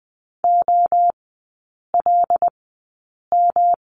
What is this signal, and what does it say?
Morse code, heard as a keyed tone, about 20 wpm, spaced as for 8 wpm, 700 Hz. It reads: OLM
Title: 107 Lesson 18e - Letter m - Three Characters 20wpm